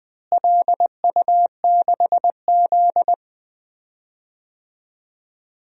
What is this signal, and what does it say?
Morse code, heard as a keyed tone, about 20 wpm, 700 Hz. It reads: LU6Z
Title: DXCC-100 Call Signs - 4 Characters - Repeated 2x - 20wpm